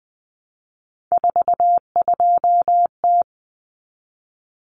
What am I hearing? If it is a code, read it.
Morse code, heard as a keyed tone, about 20 wpm, 700 Hz. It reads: 42T
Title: Common Cut Numbers - 3-Digit 20wpm